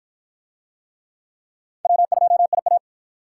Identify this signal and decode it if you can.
Morse code, heard as a keyed tone, about 48 wpm, 700 Hz. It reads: W2IA